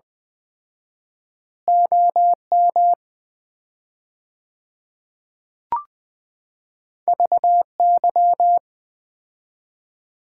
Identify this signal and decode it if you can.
Morse code, heard as a keyed tone, about 21 wpm, 700 Hz. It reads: OM VY
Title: Common QSO Phrases - 4x Word-Spacing - 20wpm